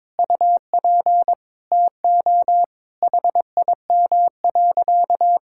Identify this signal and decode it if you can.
Morse code, heard as a keyed tone, about 22 wpm, 700 Hz. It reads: UP TO HIM.
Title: Sentences from Top 1000 Words - Review Part 6 - 22wpm